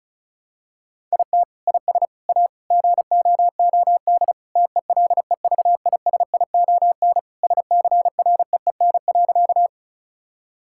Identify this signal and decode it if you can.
Morse code, heard as a keyed tone, about 36 wpm, 700 Hz. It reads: IT IS A GOOD TELEVISION SCREEN.
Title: Sentences from Top 1000 Words - Review Part 6 - 35wpm